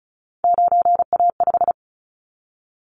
Morse code, heard as a keyed tone, about 34 wpm, 700 Hz. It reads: 9A5
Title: Callsign Prefixes for Top 100 DXCC - 35wpm